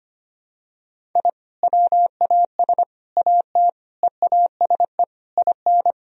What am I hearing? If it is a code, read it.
Morse code, heard as a keyed tone, about 25 wpm, 700 Hz. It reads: I WAS AT EASE IN
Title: Sentences from Top 1000 Words - Review Part 1 - 25wpm